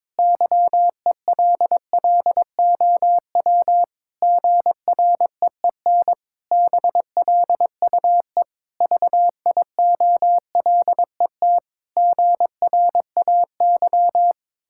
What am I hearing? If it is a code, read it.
Morse code, heard as a keyed tone, about 22 wpm, 700 Hz. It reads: YELLOW GREEN BLUE VIOLET GRAY
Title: Pangram Challenge v2 - 22wpm